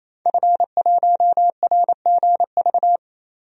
Morse code, heard as a keyed tone, about 28 wpm, 700 Hz. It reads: F1RGV
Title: DXCC-100 Call Signs - 5 Characters - 28wpm